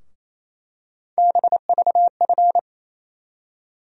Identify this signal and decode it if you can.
Morse code, heard as a keyed tone, about 28 wpm, 700 Hz. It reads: BVF
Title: USA Call Sign Postfixes - 3 Letters 28wpm